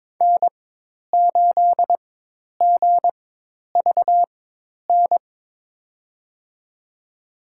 Morse code, heard as a keyed tone, about 22 wpm, spaced as for 10 wpm, 700 Hz. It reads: N8GVN
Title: USA Call Signs - 5 Characters - 4x Character Spacing - 22wpm